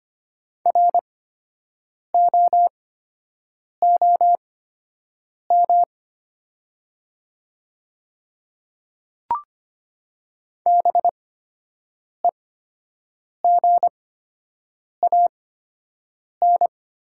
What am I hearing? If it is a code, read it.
Morse code, heard as a keyed tone, about 25 wpm, spaced as for 7 wpm, 700 Hz. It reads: ROOM BEGAN
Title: 8x Character Spacing - Top 500 Words - 25wpm